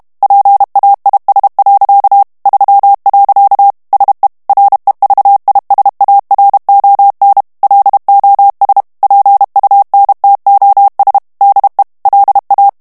Morse code, heard as a keyed tone, about 32 wpm, 800 Hz. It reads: PAIS. 3. SE REVISARON LOS PUNTOS DE LA
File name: morse mp3 / ESPAGNOL TEXTE EN CLAIR 1 VITESSE 30 MOTS MINUTES